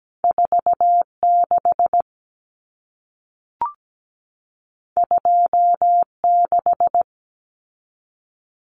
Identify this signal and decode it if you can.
Morse code, heard as a keyed tone, about 17 wpm, 700 Hz. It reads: 46 26
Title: Common Cut Numbers - 2-Digit 17wpm